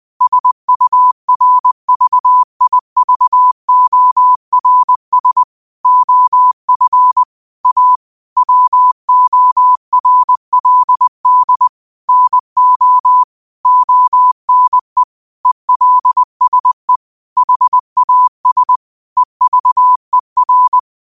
Morse code, heard as a keyed tone, about 20 wpm, 1000 Hz. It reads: SURVIVORS OF A WORLD NO ONE ELSE HAS EVER